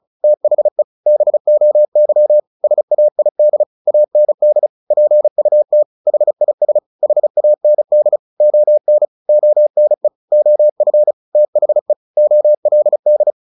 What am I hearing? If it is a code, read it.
Morse code, heard as a keyed tone, about 34 wpm, 600 Hz. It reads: THE BOY SAID AND PUT HIS HAND ON ONE OF THE OLD
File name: the old man and the sea / the old man and the sea 1 35wpm